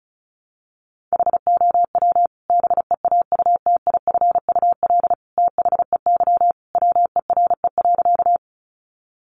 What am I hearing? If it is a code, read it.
Morse code, heard as a keyed tone, about 34 wpm, 700 Hz. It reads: HOW BEAUTIFUL THEY WERE.
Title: Sentences from Top 900 Words 35wpm